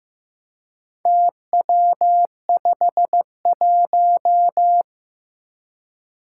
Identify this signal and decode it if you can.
Morse code, heard as a keyed tone, about 15 wpm, 700 Hz. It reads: TW51
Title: Four Challenging Letter-Number Combinations 15wpm